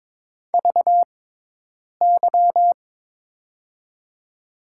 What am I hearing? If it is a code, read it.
Morse code, heard as a keyed tone, about 21 wpm, spaced as for 8 wpm, 700 Hz. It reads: VY
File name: Top 90 QSO Elements - 6x Character Spacing - 22wpm